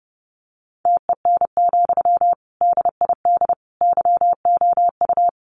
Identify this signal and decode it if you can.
Morse code, heard as a keyed tone, about 30 wpm, 700 Hz. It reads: TEN, DID YOU
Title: Sentences from Top 400 Words 30wpm